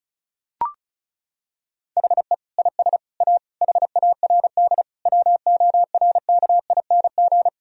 Morse code, heard as a keyed tone, about 36 wpm, 700 Hz. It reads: HE IS A HARD WORKING
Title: Sentences from Top 1000 Words - Review Part 4 - 35wpm